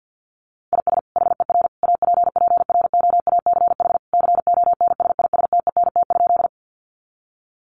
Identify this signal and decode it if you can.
Morse code, heard as a keyed tone, about 48 wpm, 700 Hz. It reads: IS HER APPROACH CONSISTENT?
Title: Sentences from Top 1500 Words 50wpm